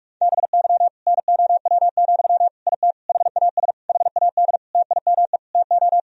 Morse code, heard as a keyed tone, about 46 wpm, 700 Hz. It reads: BY NOW, IT HAS HAD TIME TO
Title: Sentences from Top 1000 Words - Review Part 1 - 45wpm